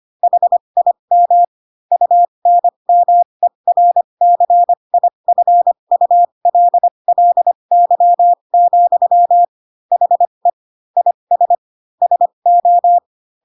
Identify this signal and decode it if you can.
Morse code, heard as a keyed tone, about 25 wpm, spaced as for 22 wpm, 700 Hz. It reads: HIM UNMERCIFULLY, HE IS SO
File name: Pinocchio / Pinocchio ch3 18 25 ik1hge